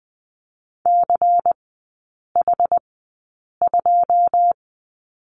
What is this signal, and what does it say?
Morse code, heard as a keyed tone, about 20 wpm, spaced as for 8 wpm, 700 Hz. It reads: CH2
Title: 127 Lesson 21e - Number 6 - Three Characters 20wpm